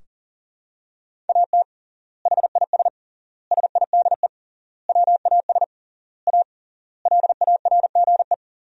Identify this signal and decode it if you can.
Morse code, heard as a keyed tone, about 40 wpm, 700 Hz. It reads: AT HIS SIDE WAS A LARGE
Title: 3x Word-Spacing - Sentences from Top 500 Words - 40wpm